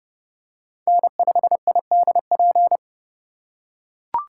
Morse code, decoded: N5IDP

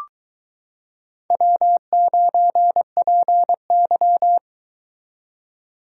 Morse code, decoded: W9PY

